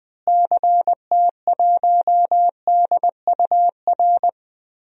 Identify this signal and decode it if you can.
Morse code, heard as a keyed tone, about 20 wpm, 700 Hz. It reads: CT1DUR
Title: DXCC-100 Call Signs - 6 Characters - Repeated 3x - 20wpm